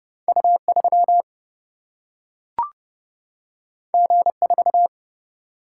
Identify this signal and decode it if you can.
Morse code, heard as a keyed tone, about 30 wpm, 700 Hz. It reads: U3 G4
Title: Single-Letter Single-Number 30wpm